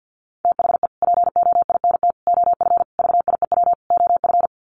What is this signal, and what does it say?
Morse code, heard as a keyed tone, about 48 wpm, 700 Hz. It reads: THE POINT OF VIEW OF
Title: Sets of 5 Words 50wpm